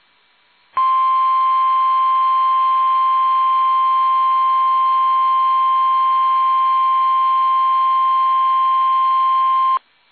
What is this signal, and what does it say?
Emergency Alert System (SAME) tones (the 1050 Hz attention tone). It is heard from 0.7 to 9.8 s.
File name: eas / KUSO / KUSO